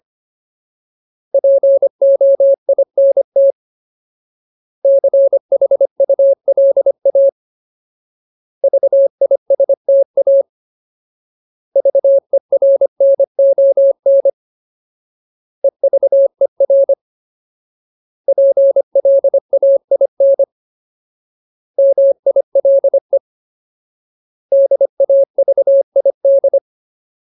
Morse code, heard as a keyed tone, about 25 wpm, 550 Hz. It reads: POINT CHULA VISTA VERNON EVER PLAIN MILE DAVID
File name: Ultimate Word List / 25WPM Ultimate